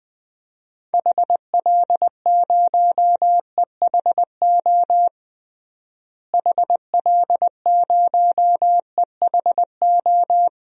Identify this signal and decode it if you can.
Morse code, heard as a keyed tone, about 20 wpm, 700 Hz. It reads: HL0EHO HL0EHO
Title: DXCC-100 Call Signs - 6 Characters - Repeated 3x - 20wpm